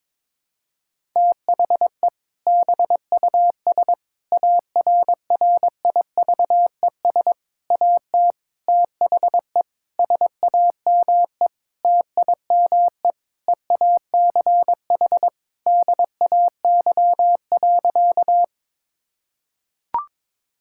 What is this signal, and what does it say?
Morse code, heard as a keyed tone, about 21 wpm, 700 Hz. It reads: THE BUS ARRIVES AT THE SAME TIME EACH DAY.
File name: Difficult Sentences with Hints - 22wpm